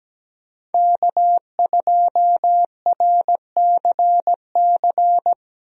Morse code, heard as a keyed tone, about 17 wpm, 700 Hz. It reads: K2RCC
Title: USA Call Signs - 5 Characters - Repeated 2x - 17wpm